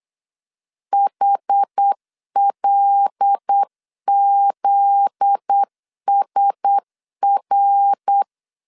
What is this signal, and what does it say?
Morse code, heard as a keyed tone, about 8 wpm, 800 Hz. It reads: HLZSR
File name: LEZIONI / 7 GF9Z / 7 GF9Z 8wpm 3213